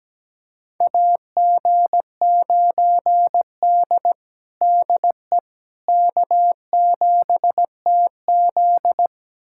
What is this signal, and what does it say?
Morse code, heard as a keyed tone, about 17 wpm, 700 Hz. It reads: AG9D DE K7TZ
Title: USA Callsign DE Callsign - 4 Characters - 17wpm